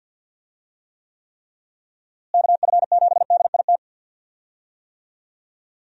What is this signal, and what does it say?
Morse code, heard as a keyed tone, about 51 wpm, 700 Hz. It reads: KF7DIT
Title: USA Call Signs - 6 Characters - Repeated 3x - 50wpm